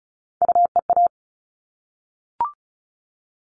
Morse code, heard as a keyed tone, about 34 wpm, 700 Hz. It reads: UEA
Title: All Cut Numbers - 3-Digit 35wpm